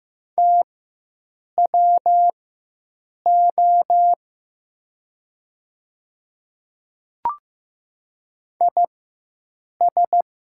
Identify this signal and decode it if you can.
Morse code, heard as a keyed tone, about 16 wpm, spaced as for 7 wpm, 700 Hz. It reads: TWO IS
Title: 4x Character Spacing - Top 100 Words - 15wpm